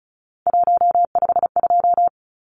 Morse code, heard as a keyed tone, about 34 wpm, 700 Hz. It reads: 152
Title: Numbers - 3-Digit 35wpm